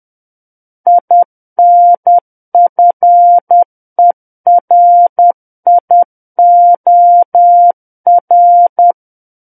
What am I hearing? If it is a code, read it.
Morse code, heard as a keyed tone, about 10 wpm, 700 Hz. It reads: INFERIOR
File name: Book One / 01-Chapter 1 - 10 WPM